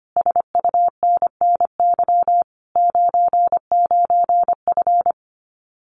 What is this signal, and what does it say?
Morse code, heard as a keyed tone, about 25 wpm, 700 Hz. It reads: SUNNY 99F
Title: Common QSO Phrases Encore 25wpm